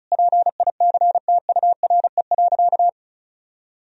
Morse code, heard as a keyed tone, about 34 wpm, 700 Hz. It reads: PICTURE.
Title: Sentences from Top 1000 Words - Review Part 1 - 35wpm